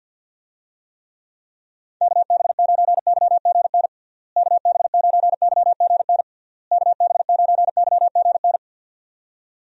Morse code, heard as a keyed tone, about 52 wpm, 700 Hz. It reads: KB9YGN KB9YGN KB9YGN
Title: USA Call Signs - 6 Characters - Repeated 3x - 50wpm